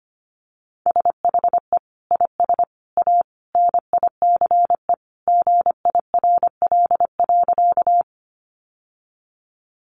Morse code, heard as a keyed tone, about 25 wpm, 700 Hz. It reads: SHE IS A NICE GIRL.
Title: Sentences from Top 1100 Words 25wpm